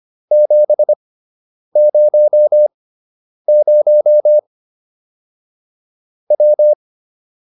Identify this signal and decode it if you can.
Morse code, heard as a keyed tone, about 25 wpm, spaced as for 9 wpm, 600 Hz. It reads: 700 W
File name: QSO 202 09